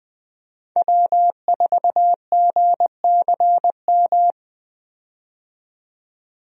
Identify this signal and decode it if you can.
Morse code, heard as a keyed tone, about 20 wpm, 700 Hz. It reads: W4GCM
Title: USA Call Signs - 5 Characters - 20wpm